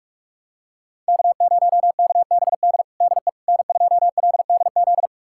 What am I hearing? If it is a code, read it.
Morse code, heard as a keyed tone, about 41 wpm, 700 Hz. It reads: K0KBD DE N2LDZ
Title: USA Callsign DE Callsign - 5 Characters - 45wpm